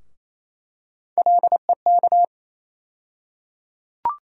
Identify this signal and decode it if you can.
Morse code, heard as a keyed tone, about 28 wpm, 700 Hz. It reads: LEK
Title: USA Call Sign Postfixes - 3 Letters 28wpm